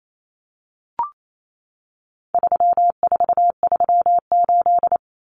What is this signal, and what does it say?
Morse code, heard as a keyed tone, about 28 wpm, 700 Hz. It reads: 3438
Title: Numbers - 4-Digit 28wpm